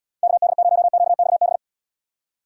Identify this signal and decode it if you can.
Morse code, heard as a keyed tone, about 79 wpm, 700 Hz. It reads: KD0QXB